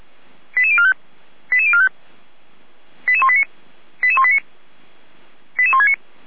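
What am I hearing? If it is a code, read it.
Five-tone selective call (ZVEI), heard as quick runs of five tones, tones of about 70 ms. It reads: 88045 88045 88189 88189 88169
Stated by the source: ZVEI five-tone selective call received by SDR.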